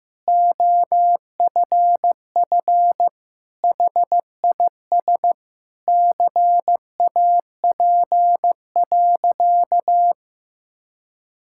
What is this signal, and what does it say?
Morse code, heard as a keyed tone, about 15 wpm, 700 Hz. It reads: OFF HIS CAP.
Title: Sentences from Top 1600 Words 15wpm